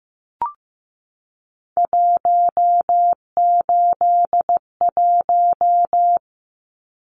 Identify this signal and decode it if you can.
Morse code, heard as a keyed tone, about 15 wpm, 700 Hz. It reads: 181
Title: Numbers - 3-Digit 15wpm